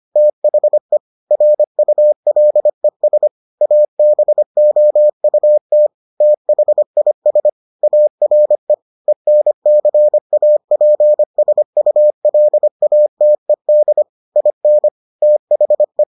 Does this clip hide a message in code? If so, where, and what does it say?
Yes: Morse code from 0.2 to 16.0 s, heard as a keyed tone, about 25 wpm, 600 Hz. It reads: THE RULES ABOUT THIS ARE ENCAPSULATED IN THE